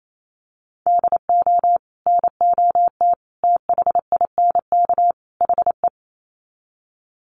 Morse code, heard as a keyed tone, about 28 wpm, 700 Hz. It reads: DO NOT THINK HE
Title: Sets of 4 Words 28wpm